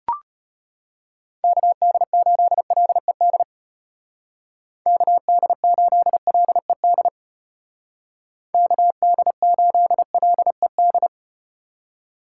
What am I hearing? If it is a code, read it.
Morse code, heard as a keyed tone, about 34 wpm, 700 Hz. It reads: KD8LED KD8LED KD8LED